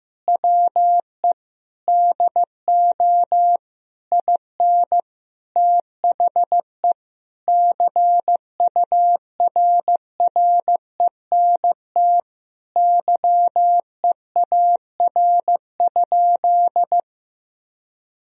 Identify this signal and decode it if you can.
Morse code, heard as a keyed tone, about 15 wpm, 700 Hz. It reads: WE DO IN THE CURRENT YEAR?
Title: Sentences from Top 1000 Words - Review Part 4 - 15wpm